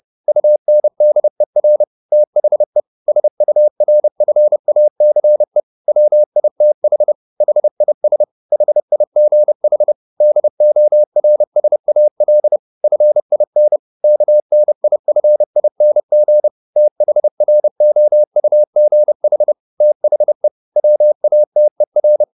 Morse code, heard as a keyed tone, about 30 wpm, 600 Hz. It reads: UNDER THE SURFACE WITH HIS HIGH DORSAL FIN KNIFING THROUGH THE WATER